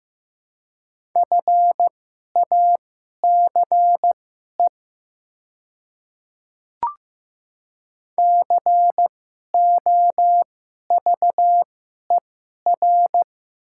Morse code, heard as a keyed tone, about 16 wpm, spaced as for 11 wpm, 700 Hz. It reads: FACE COVER
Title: 2x Character Spacing - Top 300 Words - 15wpm